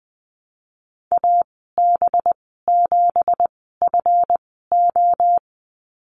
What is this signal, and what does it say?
Morse code, heard as a keyed tone, about 20 wpm, spaced as for 15 wpm, 700 Hz. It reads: AB7FO
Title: 186 Lesson 30g - Letter b - Call Signs 20wpm